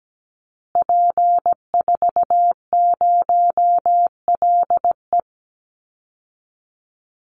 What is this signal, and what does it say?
Morse code, heard as a keyed tone, about 17 wpm, 700 Hz. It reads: P40LE
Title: Call Signs Repeated 2x - 17wpm